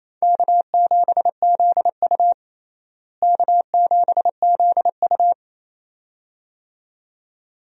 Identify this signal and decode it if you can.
Morse code, heard as a keyed tone, about 28 wpm, 700 Hz. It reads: K7ZU K7ZU
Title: USA Call Signs - 4 Characters - Repeated 2x - 28wpm